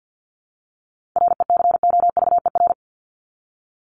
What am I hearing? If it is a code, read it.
Morse code, heard as a keyed tone, about 50 wpm, 700 Hz. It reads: RECOVER
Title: Top 1751-2000 Words 50wpm